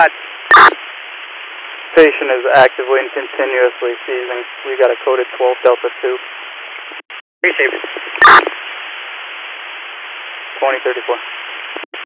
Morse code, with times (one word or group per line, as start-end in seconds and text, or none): none